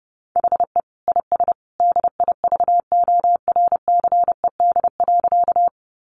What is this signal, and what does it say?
Morse code, heard as a keyed tone, about 30 wpm, 700 Hz. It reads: HE IS DIVORCED.
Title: Additional Easy Phrases 30wpm